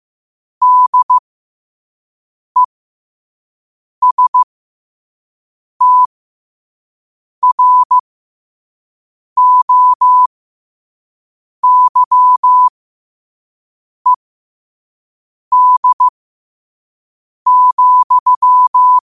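Morse code, heard as a keyed tone, about 15 wpm, spaced as for 5 wpm, 1000 Hz. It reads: DESTROYED,
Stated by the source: Morse code sent at 5 wpm.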